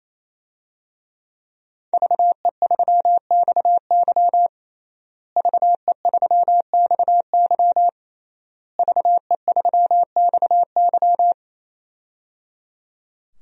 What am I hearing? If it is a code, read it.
Morse code, heard as a keyed tone, about 28 wpm, 690 Hz. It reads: VE3XY VE3XY VE3XY
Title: DXCC-100 Call Signs - 5 Characters - Repeated 3x - 28wpm